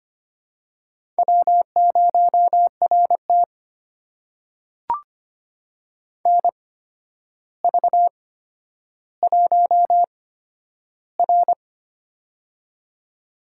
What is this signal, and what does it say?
Morse code, heard as a keyed tone, about 25 wpm, spaced as for 11 wpm, 700 Hz. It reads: W0RT NV1R
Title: USA Call Signs - 4 Characters - 8x Character Spacing - 25wpm